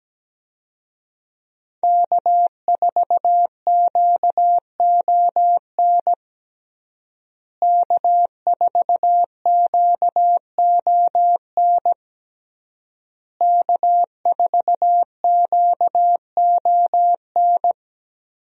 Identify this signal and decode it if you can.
Morse code, heard as a keyed tone, about 17 wpm, 700 Hz. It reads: K4QON K4QON K4QON
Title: USA Call Signs - 5 Characters - Repeated 3x - 17wpm